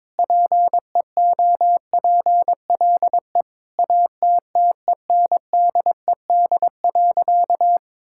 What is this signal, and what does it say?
Morse code, heard as a keyed tone, about 22 wpm, 700 Hz. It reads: PEOPLE ATTENDED.
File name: practice / 22wpm / Sentences with 4-Word Phrases - 22wpm